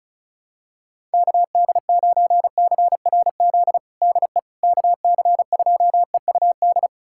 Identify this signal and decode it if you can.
Morse code, heard as a keyed tone, about 36 wpm, 700 Hz. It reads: KD9CRZ DE KC2EUD
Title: USA Callsign DE Callsign 35wpm